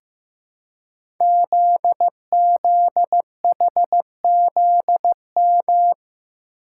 Morse code, heard as a keyed tone, about 15 wpm, 700 Hz. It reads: ZZHZM
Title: Code Groups - 5 Letters - 15wpm